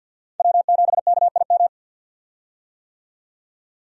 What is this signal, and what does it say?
Morse code, heard as a keyed tone, about 47 wpm, 700 Hz. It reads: W7KIM
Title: USA Call Signs - 5 Characters - 50wpm